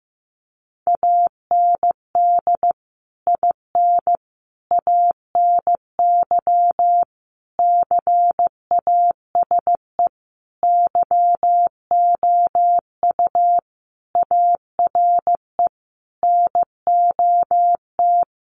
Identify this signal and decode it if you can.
Morse code, heard as a keyed tone, about 15 wpm, 700 Hz. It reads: AND IN ANY CASE YOU ARE NOT
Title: Sentences from Top 1200 Words 15wpm